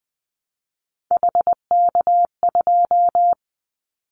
Morse code, heard as a keyed tone, about 19 wpm, 700 Hz. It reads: HK2
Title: Callsign Prefixes for Top 100 DXCC - 20wpm